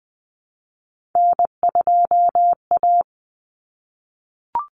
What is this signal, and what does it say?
Morse code, heard as a keyed tone, about 20 wpm, 700 Hz. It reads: N2A
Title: Common Cut Numbers - 3-Digit 20wpm